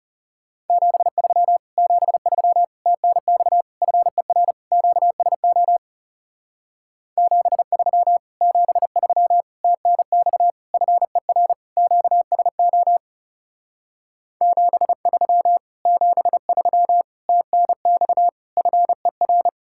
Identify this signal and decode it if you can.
Morse code, heard as a keyed tone, about 36 wpm, 700 Hz. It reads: 73 73 TNX FER QSO 73 73 TNX FER QSO 73 73 TNX FER
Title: Speed-Racing Common QSO Phrases - 30wpm